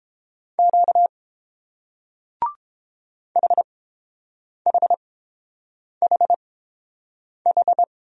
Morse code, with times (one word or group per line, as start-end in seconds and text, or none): 0.6-1.1 Q
3.4-7.8 HHHH